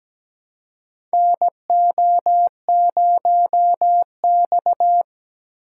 Morse code, heard as a keyed tone, about 17 wpm, 700 Hz. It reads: NO0X
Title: USA Call Signs - 6x Character Spacing - 17wpm